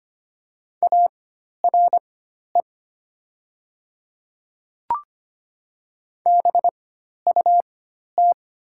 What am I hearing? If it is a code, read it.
Morse code, heard as a keyed tone, about 25 wpm, spaced as for 12 wpm, 700 Hz. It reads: ARE BUT
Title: 4x Character Spacing - Top 100 Words - 25wpm